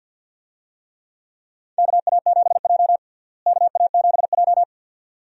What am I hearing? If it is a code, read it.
Morse code, heard as a keyed tone, about 53 wpm, 700 Hz. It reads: KA7J KA7J
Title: USA Call Signs - 4 Characters - Repeated 2x - 50wpm